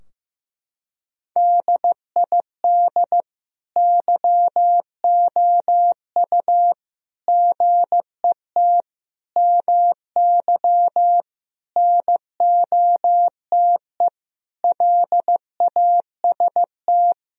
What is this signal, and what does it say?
Morse code, heard as a keyed tone, about 15 wpm, 700 Hz. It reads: DID YOU GET MY NOTE LAST
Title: Sentences from Top 1000 Words - Review Part 2 - 15wpm